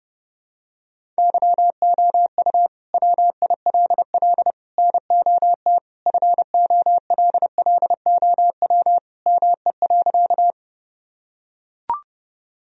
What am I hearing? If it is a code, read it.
Morse code, heard as a keyed tone, about 30 wpm, 700 Hz. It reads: YOU WILL NOT FOLLOW ME.
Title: Sentences from Top 1000 Words - Review Part 1 - 30wpm